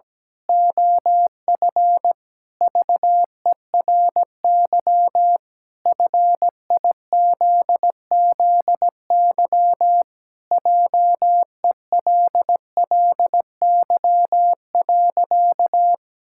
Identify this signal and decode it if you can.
Morse code, heard as a keyed tone, about 17 wpm, 700 Hz. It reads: OF VERY FIZZY JELLY.
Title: Pangram Challenge v2 - 17wpm